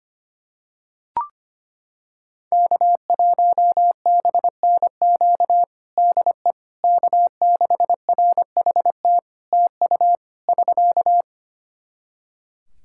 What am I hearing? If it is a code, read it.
Morse code, heard as a keyed tone, about 25 wpm, 700 Hz. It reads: K1BNQ DE K6RHT TU <SK>
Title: K1USN SST QSO Practice 25wpm